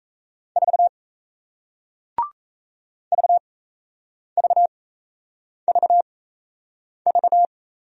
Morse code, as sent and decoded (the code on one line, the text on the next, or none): ....- / ...- ...- ...- ...-
4 VVVV